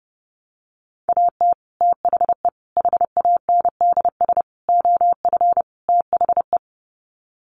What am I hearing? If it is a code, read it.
Morse code, heard as a keyed tone, about 30 wpm, 700 Hz. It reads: AT THE HANDS OF THE
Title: Sets of 5 Words Encore2 30wpm